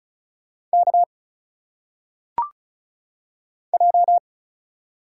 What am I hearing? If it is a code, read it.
Morse code, heard as a keyed tone, about 35 wpm, 700 Hz. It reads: K J
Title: Single Letters - Phonetic Pronunciation 35wpm